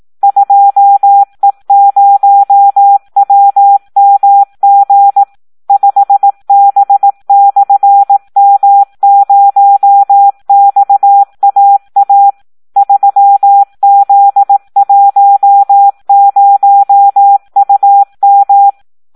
Morse code, heard as a keyed tone, about 18 wpm, 800 Hz. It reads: 2E0WMG 5B/M0XAA 3Z10UM
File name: Audio practica CW / SD 1 18